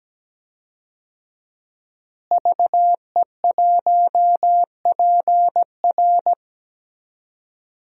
Morse code, heard as a keyed tone, about 17 wpm, 700 Hz. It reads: VE1PR
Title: DXCC-100 Call Signs - 5 Characters - Repeated 3x - 17wpm